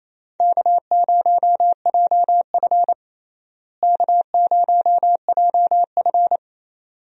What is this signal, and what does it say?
Morse code, heard as a keyed tone, about 28 wpm, 700 Hz. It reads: K0JF K0JF
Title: USA Call Signs - 4 Characters - Repeated 2x - 28wpm